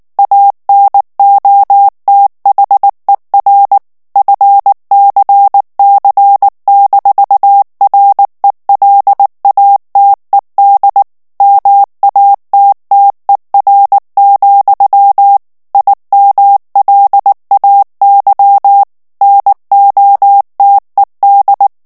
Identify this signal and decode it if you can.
Morse code, heard as a keyed tone, about 19 wpm, 800 Hz. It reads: ANOTHER FCC-RELATED MATTER, IMLAY NOTED
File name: ANGLAIS texte clair 3 vitesse 18 mots minute